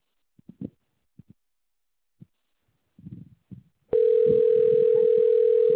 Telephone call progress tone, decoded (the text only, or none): ringback tone